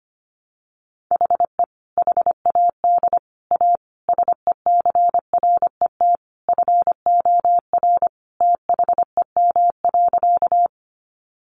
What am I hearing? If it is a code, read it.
Morse code, heard as a keyed tone, about 25 wpm, 700 Hz. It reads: HE HAD A SECRET FOR THEM.
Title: Sentences from Top 1200 Words 25wpm